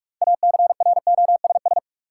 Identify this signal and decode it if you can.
Morse code, heard as a keyed tone, about 44 wpm, 700 Hz. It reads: ACROSS
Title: Sets of 2 Words - New Word Drill - 45wpm